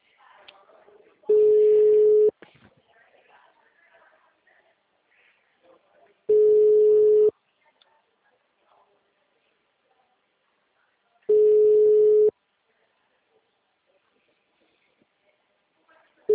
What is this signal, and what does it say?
Ringback tone (Europe, 425 Hz).
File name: out- -2001- - - .4